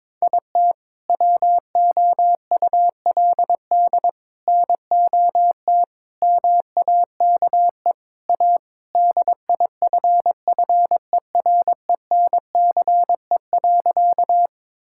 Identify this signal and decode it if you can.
Morse code, heard as a keyed tone, about 22 wpm, 700 Hz. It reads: IT WOULD NOT MAKE A DIFFERENCE.
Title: Sentences from Top 1000 Words - Review Part 5 - 22wpm